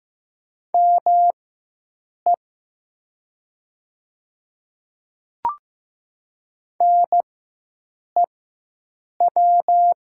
Morse code, heard as a keyed tone, about 15 wpm, spaced as for 7 wpm, 700 Hz. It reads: ME NEW
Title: 4x Character Spacing - Top 200 Words - 15wpm